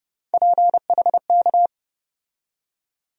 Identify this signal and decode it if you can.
Morse code, heard as a keyed tone, about 30 wpm, 700 Hz. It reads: PHK
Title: USA Call Sign Postfixes - 3 Letters 30wpm